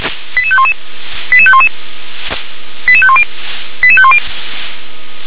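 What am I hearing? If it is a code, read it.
Five-tone selective call (ZVEI), heard as quick runs of five tones, tones of about 70 ms. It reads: 88511 88511 88510 88510
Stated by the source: ZVEI five-tone selective call received on SDR.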